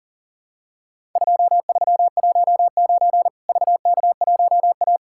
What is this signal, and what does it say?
Morse code, heard as a keyed tone, about 38 wpm, 700 Hz. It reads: 2319 VK1A
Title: Worked All Europe DX Contest - QTC Exchange - 40wpm